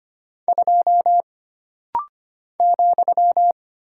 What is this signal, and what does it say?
Morse code, heard as a keyed tone, about 25 wpm, 700 Hz. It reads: 2 ,
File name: Single Character (Letter:Number:Punctuation:Prosign) - Rapid-Fire - 25wpm